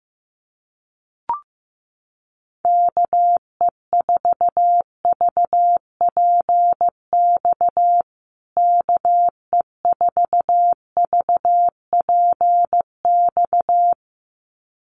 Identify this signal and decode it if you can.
Morse code, heard as a keyed tone, about 14 wpm, 700 Hz. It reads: KE4VPX KE4VPX
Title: USA Call Signs Repeated 2x 15wpm